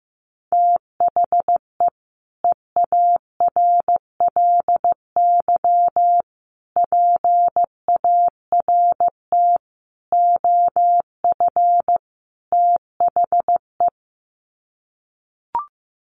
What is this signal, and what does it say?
Morse code, heard as a keyed tone, about 14 wpm, 700 Hz. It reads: THE EARLY PART OF THE
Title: Sets of 5 Words Encore2 15wpm